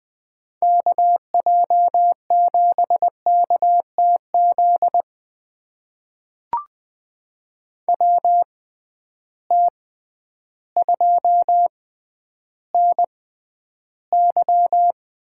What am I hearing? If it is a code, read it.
Morse code, heard as a keyed tone, about 21 wpm, spaced as for 11 wpm, 700 Hz. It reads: KJ7KTZ WT2NY
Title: USA Call Signs - 6x Character Spacing - 20wpm